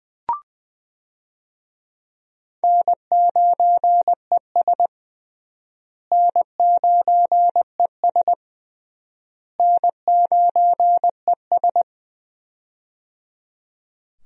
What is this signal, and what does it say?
Morse code, heard as a keyed tone, about 21 wpm, 700 Hz. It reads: N9ES N9ES N9ES
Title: USA Call Signs - 4 Characters - Repeated 3x - 20wpm